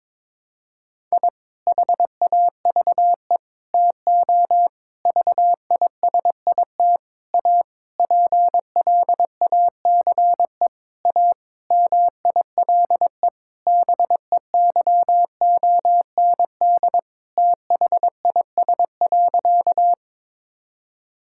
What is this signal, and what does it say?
Morse code, heard as a keyed tone, about 21 wpm, 700 Hz. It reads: I HAVE TO VISIT A PLACE A MILE BEYOND THIS.
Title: Sentences from Top 1000 Words - Review Part 6 - 22wpm